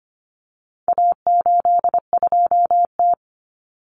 Morse code, heard as a keyed tone, about 25 wpm, 700 Hz. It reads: A82T
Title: Common Cut Numbers - 4-Digit 25wpm